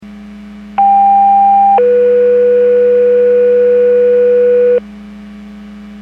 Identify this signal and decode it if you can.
Two-tone page, heard as a one-second tone, then a three-second tone, about 800 then 490 Hz, with noise. It is heard from 0.8 to 4.8 s.
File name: Portage 154.385 Paging